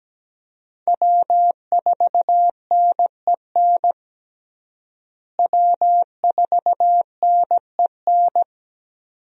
Morse code, heard as a keyed tone, about 17 wpm, 700 Hz. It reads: W4NEN W4NEN
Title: USA Call Signs - 5 Characters - Repeated 2x - 17wpm